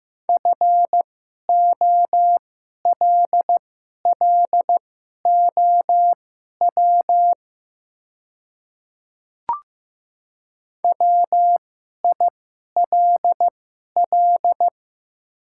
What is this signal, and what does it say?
Morse code, heard as a keyed tone, about 15 wpm, spaced as for 11 wpm, 700 Hz. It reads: FOLLOW WILL
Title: 2x Character Spacing - Top 300 Words - 15wpm